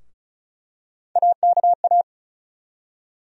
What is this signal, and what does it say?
Morse code, heard as a keyed tone, about 36 wpm, 700 Hz. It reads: AKA
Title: USA Call Sign Postfixes - 3 Letters 35wpm